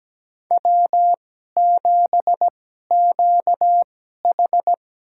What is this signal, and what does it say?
Morse code, heard as a keyed tone, about 17 wpm, spaced as for 12 wpm, 700 Hz. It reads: W7QH
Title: USA Call Signs - 4 Characters - 2x Character Spacing - 17wpm